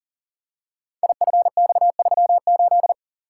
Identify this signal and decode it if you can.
Morse code, heard as a keyed tone, about 42 wpm, 690 Hz. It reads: IFX38
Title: Code Groups - 5 Letter:Numbers - 40wpm